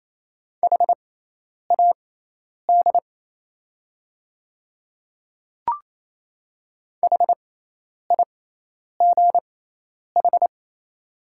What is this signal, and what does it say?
Morse code, heard as a keyed tone, about 28 wpm, spaced as for 10 wpm, 700 Hz. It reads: HAD HIGH
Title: 6x Character Spacing - Top 300 Words - 28wpm